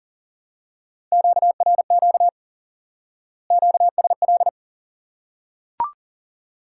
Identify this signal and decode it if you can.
Morse code, heard as a keyed tone, about 40 wpm, 700 Hz. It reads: QRQ QSL
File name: IQE Course 3 - Lesson 89 - YAGI - Two Elements - 40wpm